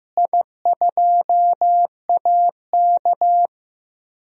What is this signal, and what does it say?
Morse code, heard as a keyed tone, about 15 wpm, 700 Hz. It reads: I2AK
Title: DXCC-100 Call Signs - 4 Characters - Repeated 3x - 15wpm